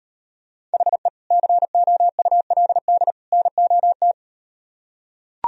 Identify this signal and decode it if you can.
Morse code, heard as a keyed tone, about 38 wpm, 700 Hz. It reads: HE COULD NOT